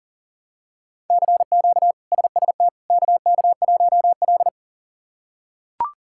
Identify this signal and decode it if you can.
Morse code, heard as a keyed tone, about 37 wpm, 700 Hz. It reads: CQ SST KK1L
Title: K1USN SST QSO Practice - 40wpm